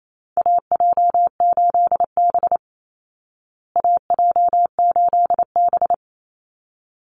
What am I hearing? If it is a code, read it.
Morse code, heard as a keyed tone, about 28 wpm, 700 Hz. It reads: AJ8B AJ8B
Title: Call Signs Repeated 3x - 28wpm